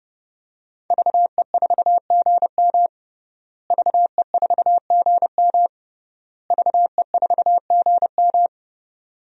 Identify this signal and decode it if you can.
Morse code, heard as a keyed tone, about 30 wpm, 700 Hz. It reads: VE4GM VE4GM VE4GM
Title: DXCC-100 Call Signs - 5 Characters - Repeated 3x - 30wpm